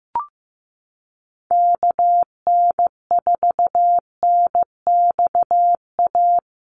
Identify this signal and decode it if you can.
Morse code, heard as a keyed tone, about 15 wpm, 700 Hz. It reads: KN4NXA
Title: USA Call Signs Repeated 3x 15wpm